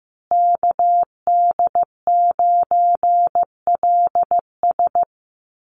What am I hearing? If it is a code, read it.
Morse code, heard as a keyed tone, about 15 wpm, 700 Hz. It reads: KD9LS
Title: USA Call Signs Repeated 2x 15wpm